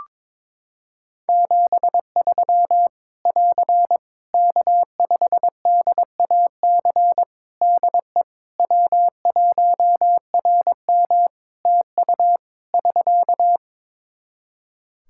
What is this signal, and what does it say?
Morse code, heard as a keyed tone, about 22 wpm, 700 Hz. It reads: 73 <AR> K5DAC DE W1RM TU <SK>
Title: Practice QSO - Exchange 4 - Ending - 22wpm